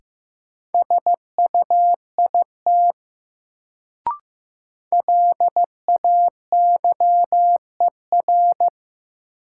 Morse code, heard as a keyed tone, about 15 wpm, 700 Hz. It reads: SUIT LAYER
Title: Sentences from Top 1600 Words - New Word Drill - 15wpm